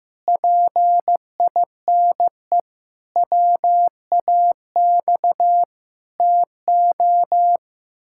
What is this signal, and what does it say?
Morse code, heard as a keyed tone, about 15 wpm, 700 Hz. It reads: PINE WAX TO
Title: Pangram Challenge v2 - 15wpm